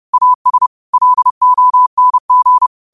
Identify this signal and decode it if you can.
Morse code, heard as a keyed tone, about 30 wpm, 1000 Hz. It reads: AS LONG
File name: qotdfiles / 30wpm